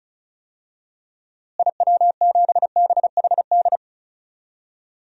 Morse code, heard as a keyed tone, about 36 wpm, 700 Hz. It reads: IW7BHD
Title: DXCC-100 Call Signs - 6 Characters - Repeated 3x - 35wpm